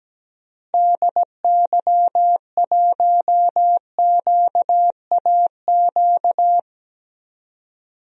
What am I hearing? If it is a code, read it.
Morse code, heard as a keyed tone, about 17 wpm, 700 Hz. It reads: DY1QAQ
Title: DXCC-100 Call Signs - 6 Characters - Repeated 2x - 17wpm